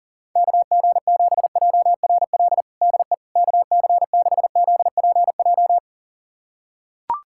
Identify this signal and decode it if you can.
Morse code, heard as a keyed tone, about 37 wpm, 700 Hz. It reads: KG7JRL DE KC6ZPJ
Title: USA Callsign DE Callsign - 6 Characters - 40wpm